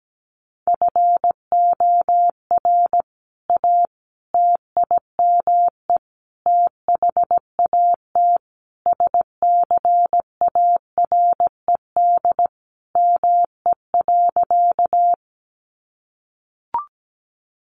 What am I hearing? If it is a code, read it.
Morse code, heard as a keyed tone, about 17 wpm, 700 Hz. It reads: FOR A TIME THAT SCARED ME.
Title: Sentences from Top 700 Words 17wpm